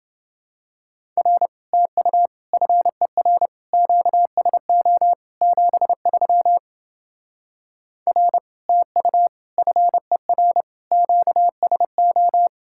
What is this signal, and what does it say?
Morse code, heard as a keyed tone, about 29 wpm, 700 Hz. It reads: R TU FER QSO 73 R TU FER QSO